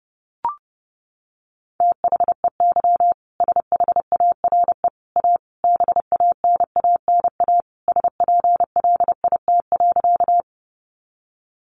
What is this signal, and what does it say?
Morse code, heard as a keyed tone, about 30 wpm, 700 Hz. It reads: THEY SHARE A BANANA SPLIT.
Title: Additional Easy Phrases 30wpm